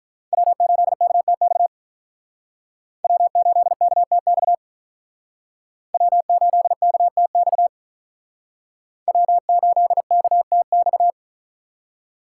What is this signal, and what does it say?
Morse code, heard as a keyed tone, about 45 wpm, 700 Hz. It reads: W8KTX W8KTX W8KTX W8KTX